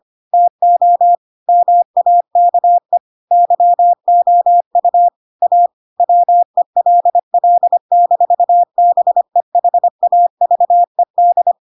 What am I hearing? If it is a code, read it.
Morse code, heard as a keyed tone, about 25 wpm, 700 Hz. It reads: TO MAKE YOU A WELL-BEHAVED
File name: Pinocchio / Pinocchio ch3 25 25 ik1hge